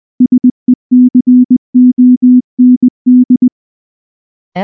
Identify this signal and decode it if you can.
Morse code, heard as a keyed tone, about 19 wpm, 250 Hz. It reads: SECOND
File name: morse / morse-training-20WPM-8